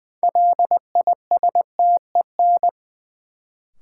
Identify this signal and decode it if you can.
Morse code, heard as a keyed tone, about 20 wpm, 700 Hz. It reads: LISTEN
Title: Sentences from Top 400 Words - New Word Drill - 20wpm